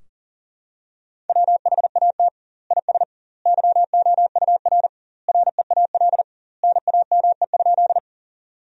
Morse code, heard as a keyed tone, about 40 wpm, 700 Hz. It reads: WHAT IS YOUR REAL NAME?
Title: 2x Word-Spacing - Sentences from Top 300 Words - 40wpm